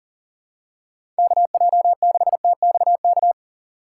Morse code, heard as a keyed tone, about 40 wpm, 700 Hz. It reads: KJ6TXK
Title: USA Call Signs - 6 Characters - 6x Character Spacing - 40wpm